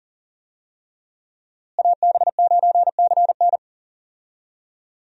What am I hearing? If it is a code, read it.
Morse code, heard as a keyed tone, about 40 wpm, 700 Hz. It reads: AB9CN
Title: USA Call Signs - 5 Characters - Repeated 3x - 40wpm